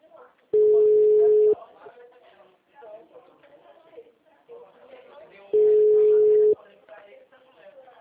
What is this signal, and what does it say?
Ringback tone (Europe, 425 Hz), with speech.
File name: out- -2015